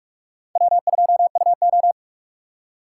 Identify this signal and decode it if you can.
Morse code, heard as a keyed tone, about 43 wpm, 700 Hz. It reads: W2UO
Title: USA Call Signs - 6x Character Spacing - 45wpm